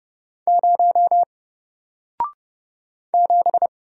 Morse code, heard as a keyed tone, about 30 wpm, 700 Hz. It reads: 0 7
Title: Single Letter-Number Alt 30wpm